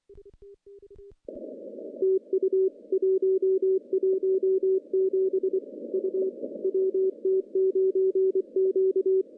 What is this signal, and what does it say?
Morse code, heard as a keyed tone, about 23 wpm, 380 Hz. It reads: TU 117 U WT9Q
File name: contests / CW-SS-k5tr- -0400